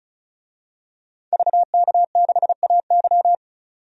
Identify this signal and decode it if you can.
Morse code, heard as a keyed tone, about 34 wpm, 700 Hz. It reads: VK6AY
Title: DXCC-100 Call Signs - 5 Characters - 35wpm